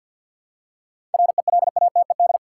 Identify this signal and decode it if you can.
Morse code, heard as a keyed tone, about 51 wpm, 700 Hz. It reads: RELATED
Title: Seven Letter Words - 50wpm